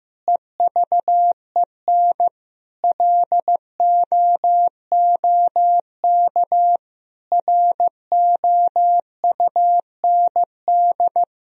Morse code, heard as a keyed tone, about 15 wpm, 700 Hz. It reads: EVEN LOOK ROUND
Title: Sentences from Top 1000 Words - Review Part 1 - 15wpm